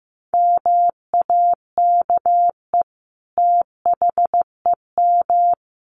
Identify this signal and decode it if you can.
Morse code, heard as a keyed tone, about 15 wpm, 700 Hz. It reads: MAKE THEM
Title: Sentences from Top 900 Words 15wpm